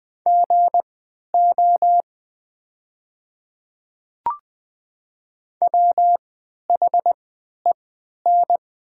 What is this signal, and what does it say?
Morse code, heard as a keyed tone, about 20 wpm, spaced as for 11 wpm, 700 Hz. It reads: GO WHEN
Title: 3x Character Spacing - Top 100 Words - 20wpm